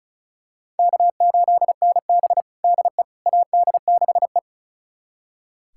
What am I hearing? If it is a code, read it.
Morse code, heard as a keyed tone, about 34 wpm, 700 Hz. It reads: K8NB DE AD6E
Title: USA Callsign DE Callsign - 4 Characters - 35wpm